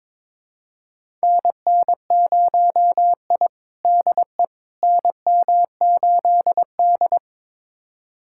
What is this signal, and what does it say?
Morse code, heard as a keyed tone, about 22 wpm, 700 Hz. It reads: NN0I DE NM8D
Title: USA Callsign DE Callsign - 4 Characters - 22wpm